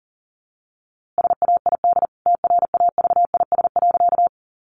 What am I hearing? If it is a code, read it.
Morse code, heard as a keyed tone, about 40 wpm, 700 Hz. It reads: SAID TRAVIS.
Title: A Sound Of Thunder 40wpm